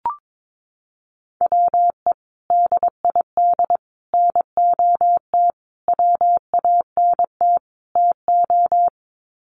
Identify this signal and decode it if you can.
Morse code, heard as a keyed tone, about 22 wpm, 700 Hz. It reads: WE DID NOT WANT TO
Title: Sets of 5 Words Encore 22wpm